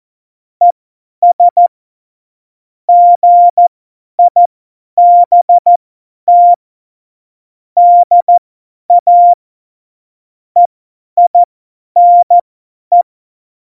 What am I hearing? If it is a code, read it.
Morse code, heard as a keyed tone, about 14 wpm, spaced as for 10 wpm, 700 Hz. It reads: ES GIBT DA EINE
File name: Morsekurs / Witze / Witz 13 March 2018, 50 F70